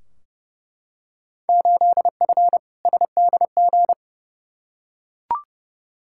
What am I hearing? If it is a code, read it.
Morse code, heard as a keyed tone, about 30 wpm, 700 Hz. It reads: 8F SDG
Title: Field Day Exchanges Encore - 30wpm